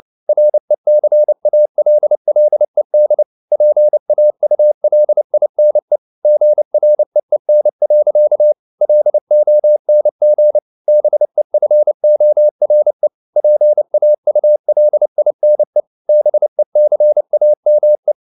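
Morse code, heard as a keyed tone, about 29 wpm, 600 Hz. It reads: RECALLED PAULINE GREEN. LONG BEFORE PAULINE BECAME
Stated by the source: Morse code at 29 wpm, sending Ken Follett's novel Never in English, from CW med Gnister.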